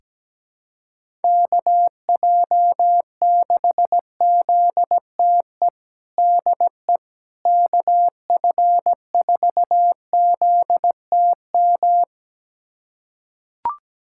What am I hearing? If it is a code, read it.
Morse code, heard as a keyed tone, about 17 wpm, 700 Hz. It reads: KJ6ZTE DE KF4ZTM
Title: USA Callsign DE Callsign - 6 Characters - 17wpm